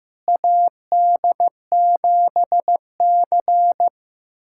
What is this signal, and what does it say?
Morse code, heard as a keyed tone, about 16 wpm, 700 Hz. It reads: AD7C
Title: USA Callsign DE Callsign - 4 Characters - 15wpm